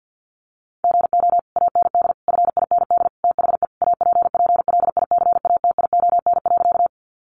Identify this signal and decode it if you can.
Morse code, heard as a keyed tone, about 50 wpm, 700 Hz. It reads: GO AND FIND THE APPLICATION.
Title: Sentences from Top 1200 Words 50wpm